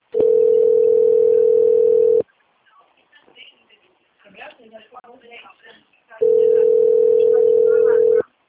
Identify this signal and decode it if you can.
Ringback tone (North America), 440 + 480 Hz, with speech.
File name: rg-1600-unknown- - - .49633